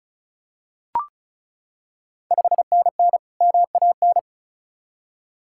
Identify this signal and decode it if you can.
Morse code, heard as a keyed tone, about 36 wpm, 690 Hz. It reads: 5NN MAN
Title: Ontario QSO Party Exchange 35wpm